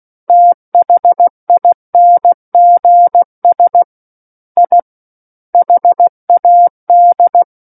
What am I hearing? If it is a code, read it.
Morse code, heard as a keyed tone, about 16 wpm, 700 Hz. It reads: THINGS I HAD
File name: Book One / 07-Chapter 7 - 16 WPM